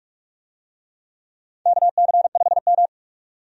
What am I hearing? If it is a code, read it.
Morse code, heard as a keyed tone, about 41 wpm, 700 Hz. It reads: KC5M
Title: USA Call Signs - 4 Characters - 45wpm